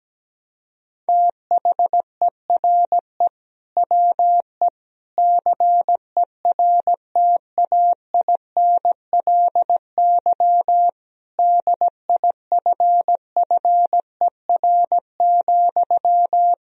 Morse code, heard as a keyed tone, about 17 wpm, 700 Hz. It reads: THERE WE CERTAINLY DIFFER,
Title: Sentences from Top 1000 Words - Review Part 1 - 17wpm